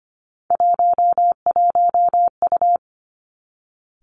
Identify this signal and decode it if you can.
Morse code, heard as a keyed tone, about 25 wpm, 700 Hz. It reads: 11U
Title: Three Challenging Letter-Number Combinations 25wpm